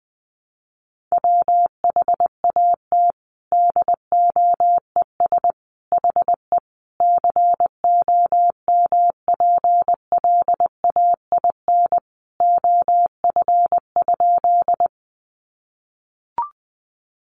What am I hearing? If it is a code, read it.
Morse code, heard as a keyed tone, about 20 wpm, 700 Hz. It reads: WHAT DOES HE COMPLAIN OF?
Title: Sentences from Top 1300 Words 20wpm